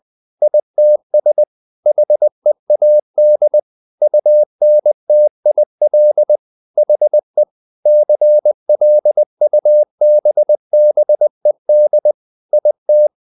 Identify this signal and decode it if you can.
Morse code, heard as a keyed tone, about 20 wpm, 600 Hz. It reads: ITS HEAD UNTIL HE CLUBBED IT
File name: the old man and the sea part 2 20wpm